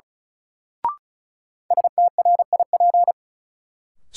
Morse code, heard as a keyed tone, about 36 wpm, 700 Hz. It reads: STRIP
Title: Sentences from Top 1600 Words - New Word Drill - 35wpm